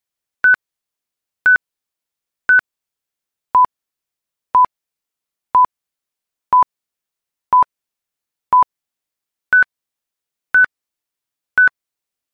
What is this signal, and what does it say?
There is no code.